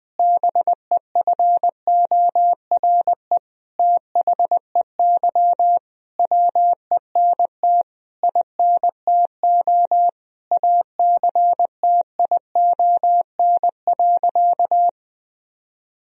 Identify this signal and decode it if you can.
Morse code, heard as a keyed tone, about 20 wpm, 700 Hz. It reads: BEFORE THEY WENT INTO ACTION.
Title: Sentences from Top 1000 Words - Review Part 4 - 20wpm